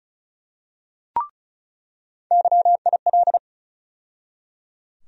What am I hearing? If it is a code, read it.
Morse code, heard as a keyed tone, about 36 wpm, 700 Hz. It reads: YIL
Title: USA Call Sign Postfixes - 3 Letters 35wpm